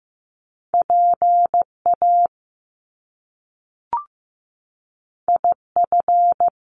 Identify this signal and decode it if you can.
Morse code, heard as a keyed tone, about 15 wpm, 700 Hz. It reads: PA IF
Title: Top Two Letter Combinations in Words 15wpm